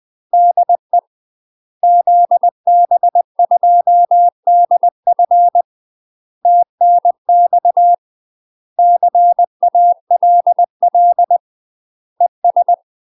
Morse code, heard as a keyed tone, about 20 wpm, 700 Hz. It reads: DE ZB2DF TNX CALL ES
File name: UK MASTER QSOs TRAINING/20WPM